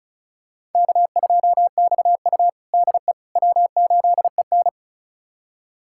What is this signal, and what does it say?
Morse code, heard as a keyed tone, about 34 wpm, 700 Hz. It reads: K2XU DE W8EN
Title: USA Callsign DE Callsign - 4 Characters - 35wpm